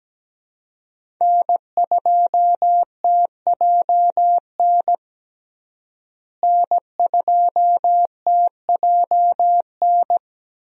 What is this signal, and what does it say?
Morse code, heard as a keyed tone, about 17 wpm, 700 Hz. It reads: N2TJN N2TJN
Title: USA Call Signs - 5 Characters - Repeated 3x - 17wpm